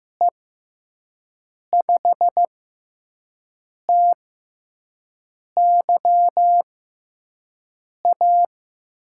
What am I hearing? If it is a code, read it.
Morse code, heard as a keyed tone, about 15 wpm, spaced as for 5 wpm, 700 Hz. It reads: E5TYA
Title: USA Call Signs - 6 Characters - 6x Character Spacing - 15wpm